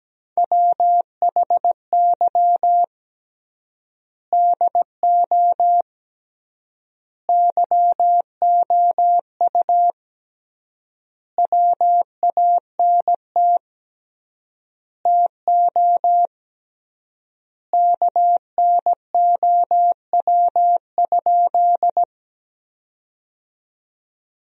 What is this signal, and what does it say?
Morse code, heard as a keyed tone, about 17 wpm, 700 Hz. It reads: WHY DO YOU WANT TO KNOW?
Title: 3x Word-Spacing - Sentences from Top 200 Words - 17wpm